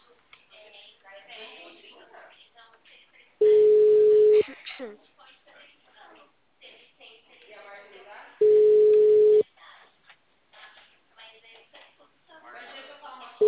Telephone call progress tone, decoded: ringback tone

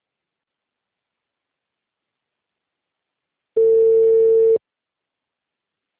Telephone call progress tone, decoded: ringback tone

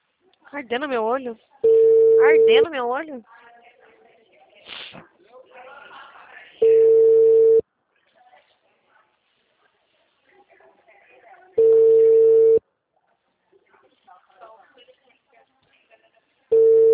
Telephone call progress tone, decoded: ringback tone

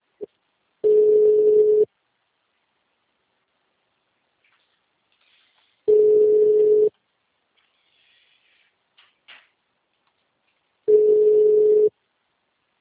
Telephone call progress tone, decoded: ringback tone

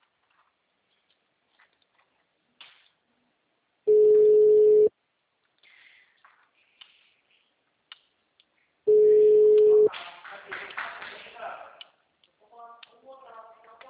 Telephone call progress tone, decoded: ringback tone

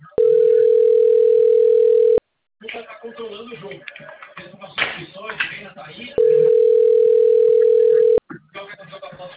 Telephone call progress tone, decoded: ringback tone

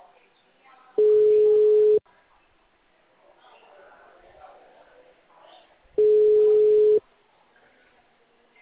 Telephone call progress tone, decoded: ringback tone

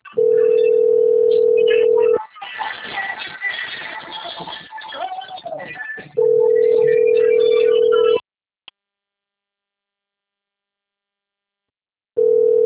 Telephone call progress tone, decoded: ringback tone

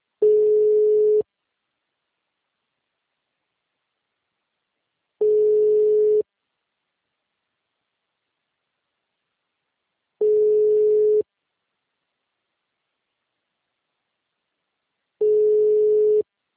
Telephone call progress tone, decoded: ringback tone

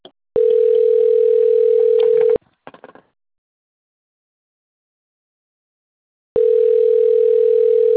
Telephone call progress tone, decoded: ringback tone

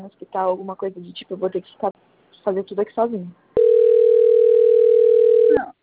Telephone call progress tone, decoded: ringback tone

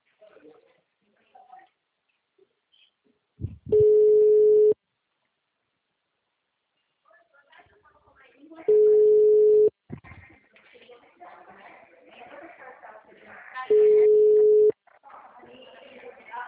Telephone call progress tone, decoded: ringback tone

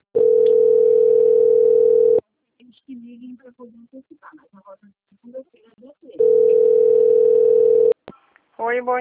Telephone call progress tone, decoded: ringback tone

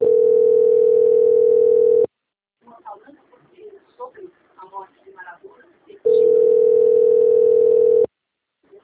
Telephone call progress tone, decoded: ringback tone

